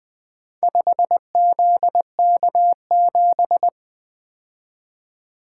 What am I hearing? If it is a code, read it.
Morse code, heard as a keyed tone, about 24 wpm, 700 Hz. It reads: 5ZK7
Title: Speed-Racing - Four Letter-Number Combinations - 20wpm